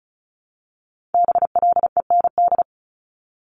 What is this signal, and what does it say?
Morse code, heard as a keyed tone, about 36 wpm, 700 Hz. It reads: BLEND
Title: Multi-letter Phonemes 35wpm